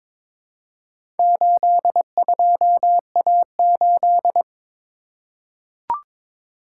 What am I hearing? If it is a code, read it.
Morse code, heard as a keyed tone, about 22 wpm, 700 Hz. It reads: 82A8
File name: practice / 22wpm / Four Letter-Number Combinations - 22wpm